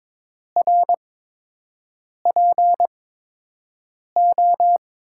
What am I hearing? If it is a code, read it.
Morse code, heard as a keyed tone, about 22 wpm, spaced as for 6 wpm, 700 Hz. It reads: RPO
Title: USA Call Signs - 5 Characters - 8x Character Spacing - 22wpm